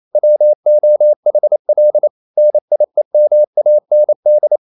Morse code, heard as a keyed tone, about 28 wpm, 600 Hz. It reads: WOHL NIEMAND
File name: Liebespaar4 28wpm